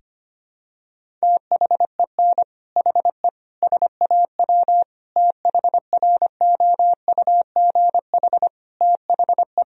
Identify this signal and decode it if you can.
Morse code, heard as a keyed tone, about 25 wpm, 700 Hz. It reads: THEN HE SAW THROUGH THE
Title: Sentences from Top 1000 Words - Review Part 1 - 25wpm